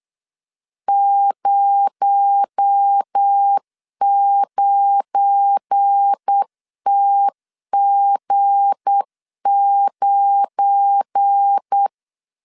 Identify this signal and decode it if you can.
Morse code, heard as a keyed tone, about 8 wpm, 800 Hz. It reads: 09TG9
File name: LEZIONI / 7 GF9Z / 7 GF9 8wpm 3545